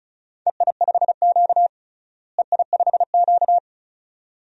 Morse code, heard as a keyed tone, about 36 wpm, 700 Hz. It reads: EI5Q EI5Q
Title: DXCC-100 Call Signs - 4 Characters - Repeated 2x - 35wpm